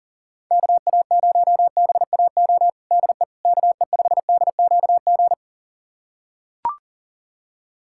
Morse code, heard as a keyed tone, about 40 wpm, 700 Hz. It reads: KA0BAO DE KE5DQG
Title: USA Callsign DE Callsign 40wpm